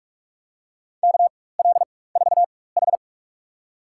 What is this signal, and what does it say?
Morse code, heard as a keyed tone, about 46 wpm, spaced as for 21 wpm, 700 Hz. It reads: KL4H
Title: USA Call Signs - 4 Characters - 4x Character Spacing - 45wpm